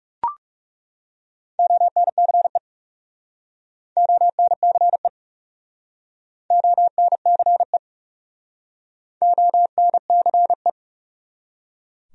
Morse code, heard as a keyed tone, about 39 wpm, 700 Hz. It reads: ONCE ONCE ONCE ONCE